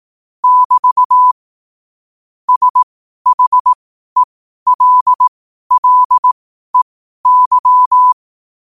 Morse code, heard as a keyed tone, about 18 wpm, spaced as for 12 wpm, 1000 Hz. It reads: <BT> SHELLEY
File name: qotdfiles / 12wpm